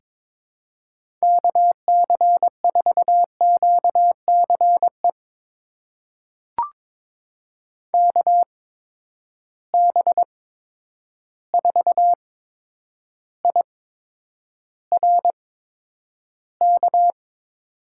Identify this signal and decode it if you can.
Morse code, heard as a keyed tone, about 22 wpm, spaced as for 9 wpm, 700 Hz. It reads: KC4QCE KB4IRK
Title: USA Call Signs - 6 Characters - 8x Character Spacing - 22wpm